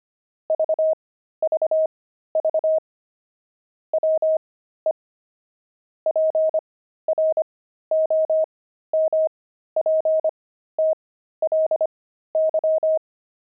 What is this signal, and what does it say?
Morse code, heard as a keyed tone, about 25 wpm, spaced as for 13 wpm, 630 Hz. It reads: VVV WE PROMPTLY
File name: pangram-fs-13-cs-25-buckles